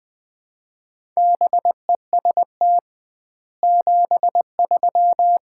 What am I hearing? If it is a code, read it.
Morse code, heard as a keyed tone, about 21 wpm, 700 Hz. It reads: BEST 73
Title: Common QSO Phrases - 2x Word-Spacing - 20wpm